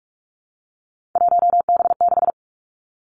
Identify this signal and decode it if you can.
Morse code, heard as a keyed tone, about 46 wpm, 700 Hz. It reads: 1B6
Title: Three Challenging Letter-Number Combinations 45wpm